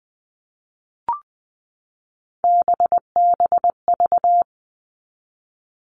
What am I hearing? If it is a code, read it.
Morse code, heard as a keyed tone, about 20 wpm, 700 Hz. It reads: BBV
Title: All Cut Numbers - 3-Digit 20wpm